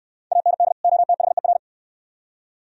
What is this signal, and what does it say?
Morse code, heard as a keyed tone, about 69 wpm, 700 Hz. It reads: AND OTHER